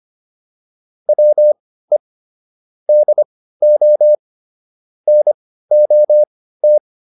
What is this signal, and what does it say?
Morse code, heard as a keyed tone, about 25 wpm, spaced as for 15 wpm, 600 Hz. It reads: WE DO NOT
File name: WD405 15